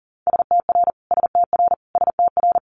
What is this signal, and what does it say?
Morse code, heard as a keyed tone, about 39 wpm, 700 Hz. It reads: STR STR STR
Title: Multi-letter Phonemes Repeated 3x 40wpm